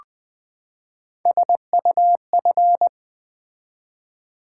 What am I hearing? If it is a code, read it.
Morse code, heard as a keyed tone, about 20 wpm, 700 Hz. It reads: SUF
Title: Build A Word - 3 Letters At A Time - 9-Letter Words - 20wpm